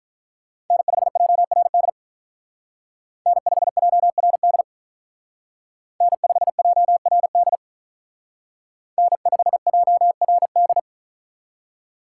Morse code, heard as a keyed tone, about 43 wpm, 700 Hz. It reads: N5JRD N5JRD N5JRD N5JRD
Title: Speed Racing - USA 1x3 Call Signs 35wpm